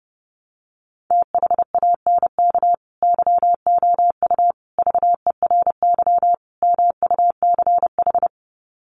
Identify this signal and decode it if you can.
Morse code, heard as a keyed tone, about 30 wpm, 700 Hz. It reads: THANK YOU VERY MUCH
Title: Sets of 4 Words 30wpm